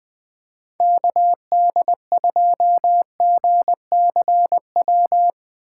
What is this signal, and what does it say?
Morse code, heard as a keyed tone, about 20 wpm, 700 Hz. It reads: KD2GCW
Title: USA Call Signs - 2x Character Spacing - 20wpm